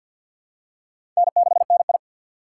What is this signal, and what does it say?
Morse code, heard as a keyed tone, about 50 wpm, 700 Hz. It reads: N6NI
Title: USA Call Signs - 4 Characters - 6x Character Spacing - 50wpm